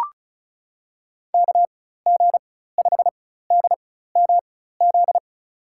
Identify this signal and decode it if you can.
Morse code, heard as a keyed tone, about 34 wpm, spaced as for 16 wpm, 700 Hz. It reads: KG5DMZ
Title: USA Call Signs - 6 Characters - 4x Character Spacing - 35wpm